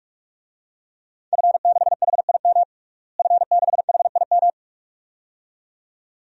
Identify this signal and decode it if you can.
Morse code, heard as a keyed tone, about 41 wpm, 700 Hz. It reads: F6HIM F6HIM
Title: DXCC-100 Call Signs - 5 Characters - Repeated 2x - 45wpm